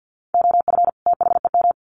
Morse code, heard as a keyed tone, about 50 wpm, 700 Hz. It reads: OF THEM
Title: Sets of 2 Words 50wpm